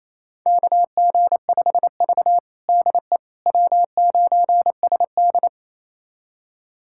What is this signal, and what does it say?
Morse code, heard as a keyed tone, about 27 wpm, 700 Hz. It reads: KG5V DE W9SD
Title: USA Callsign DE Callsign - 4 Characters - 28wpm